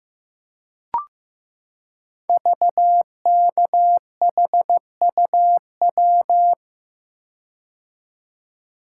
Morse code, heard as a keyed tone, about 15 wpm, 700 Hz. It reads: VKHUW
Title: Code Groups - 5 Letters - 15wpm